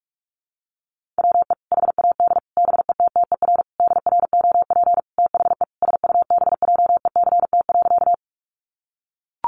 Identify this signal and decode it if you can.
Morse code, heard as a keyed tone, about 44 wpm, 700 Hz. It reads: WE HAD BETTER DROP THE SUBJECT.
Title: Sentences from Top 1400 Words 45wpm